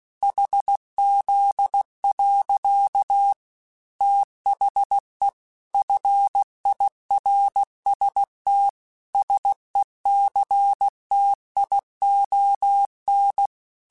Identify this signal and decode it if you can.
Morse code, heard as a keyed tone, about 16 wpm, 800 Hz. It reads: HZ. THE FIRST SECTION
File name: Technical Subjects / Technical Subjects-mp3s / Sw Pwr Sup @15